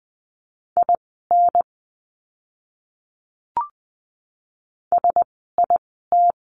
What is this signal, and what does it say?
Morse code, heard as a keyed tone, about 20 wpm, spaced as for 15 wpm, 700 Hz. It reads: IN SIT
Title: 036 Lesson 7f - Letter s - Words 20wpm